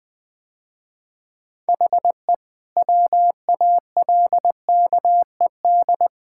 Morse code, heard as a keyed tone, about 20 wpm, 700 Hz. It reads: HE WALKED
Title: Sentences with 5-Word Phrases - Spoken Beginning - 20wpm